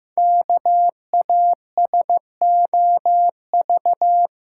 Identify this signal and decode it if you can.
Morse code, heard as a keyed tone, about 15 wpm, 700 Hz. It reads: KASOV
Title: Code Groups - 5 Letters - 15wpm